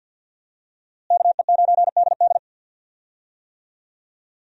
Morse code, heard as a keyed tone, about 51 wpm, 700 Hz. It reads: KE9DD
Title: USA Call Signs - 5 Characters - Repeated 3x - 50wpm